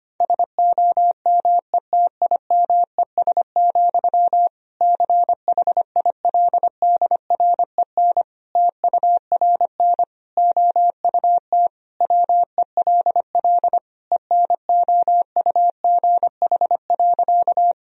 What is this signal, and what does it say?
Morse code, heard as a keyed tone, about 25 wpm, 700 Hz. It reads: SOMETIMES, CHILDREN TURN OUT WELL ENOUGH.
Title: Sentences from Top 1000 Words - Review Part 3 - 25wpm